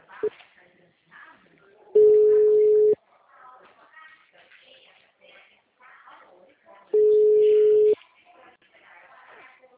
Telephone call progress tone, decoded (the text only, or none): ringback tone